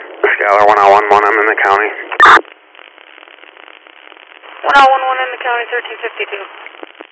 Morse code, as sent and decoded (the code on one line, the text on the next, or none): none